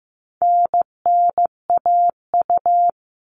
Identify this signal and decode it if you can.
Morse code, heard as a keyed tone, about 15 wpm, 700 Hz. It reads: NNAU
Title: All Cut Numbers - 4-Digit 15wpm